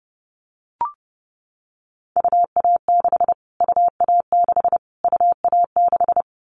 Morse code, heard as a keyed tone, about 30 wpm, 700 Hz. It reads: UA6 UA6 UA6
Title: Callsign Prefixes for Top 25 DXCC - Repeated 3x - 30wpm